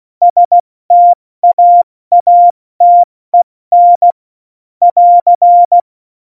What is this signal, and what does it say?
Morse code, heard as a keyed tone, about 16 wpm, 700 Hz. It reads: STAATEN <AR>
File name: Morsekurs / Witze / Witz 01. Dezember 2020, 70 F80